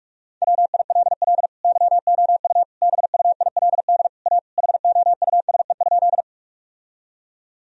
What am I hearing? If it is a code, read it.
Morse code, heard as a keyed tone, about 41 wpm, 700 Hz. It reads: WILL YOU BUILD A HOUSE?
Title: Sentences from Top 1000 Words - Review Part 1 - 45wpm